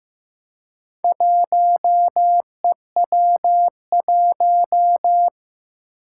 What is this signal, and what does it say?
Morse code, heard as a keyed tone, about 15 wpm, 700 Hz. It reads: 1EW1
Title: Four Challenging Letter-Number Combinations 15wpm